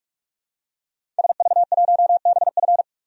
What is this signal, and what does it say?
Morse code, heard as a keyed tone, about 43 wpm, 700 Hz. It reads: SV1BF
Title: DXCC-100 Call Signs - 5 Characters - 45wpm